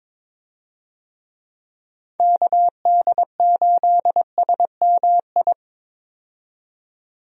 Morse code, heard as a keyed tone, about 22 wpm, 700 Hz. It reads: KD8SMI
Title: USA Call Signs - 6 Characters - 22wpm